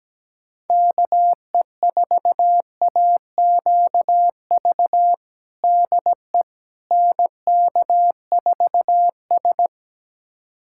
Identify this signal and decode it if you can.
Morse code, heard as a keyed tone, about 17 wpm, 700 Hz. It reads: KE4AQV DE NK4S
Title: USA Callsign DE Callsign 17wpm